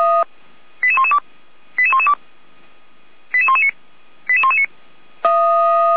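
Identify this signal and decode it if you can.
Five-tone selective call (ZVEI), heard as quick runs of five tones, tones of about 70 ms. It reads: 88112 88112 88119 88119
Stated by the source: ZVEI five-tone selective call received on an SDR.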